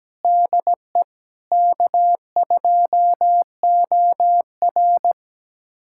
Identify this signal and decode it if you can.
Morse code, heard as a keyed tone, about 17 wpm, 700 Hz. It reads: DE K2OR
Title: USA Callsign DE Callsign 17wpm